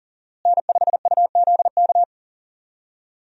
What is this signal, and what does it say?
Morse code, heard as a keyed tone, about 40 wpm, 700 Hz. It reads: N5UZK
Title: USA Call Signs - 5 Characters - Repeated 2x - 40wpm